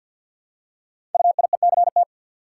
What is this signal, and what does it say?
Morse code, heard as a keyed tone, about 49 wpm, 700 Hz. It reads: UIECT
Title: Code Groups - 5 Letters - 50wpm